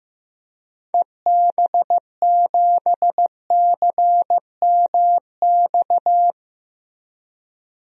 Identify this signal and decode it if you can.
Morse code, heard as a keyed tone, about 16 wpm, 700 Hz. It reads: EB7CMX
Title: DXCC-100 Call Signs - 6 Characters - 15wpm